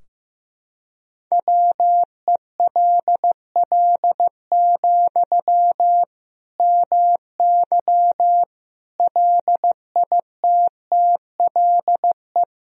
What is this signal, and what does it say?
Morse code, heard as a keyed tone, about 16 wpm, 700 Hz. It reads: WELL, MY LITTLE
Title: Sentences from Top 1000 Words - Review Part 2 - 15wpm